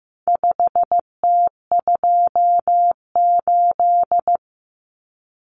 Morse code, heard as a keyed tone, about 15 wpm, 700 Hz. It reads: 5T28
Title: Common Cut Numbers - 4-Digit 15wpm